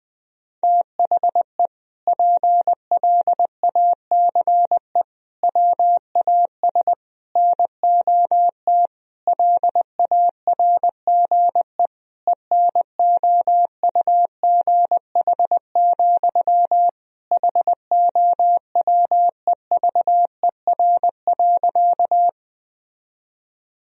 Morse code, heard as a keyed tone, about 20 wpm, 700 Hz. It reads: THE PLACE WAS NOT LARGE ENOUGH, HOWEVER.
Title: Sentences from Top 1000 Words - Review Part 1 - 20wpm